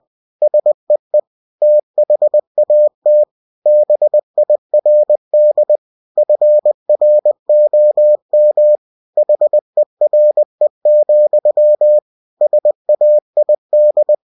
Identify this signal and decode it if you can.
Morse code, heard as a keyed tone, about 20 wpm, 600 Hz. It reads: SEE THAT BIRD FROM HERE, SAID
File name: cw-books / winnie-the-pooh / winnie-20wpm-page-0069